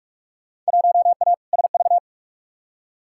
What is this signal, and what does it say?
Morse code, heard as a keyed tone, about 46 wpm, 700 Hz. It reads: 1A SV
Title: Field Day Exchanges - 45wpm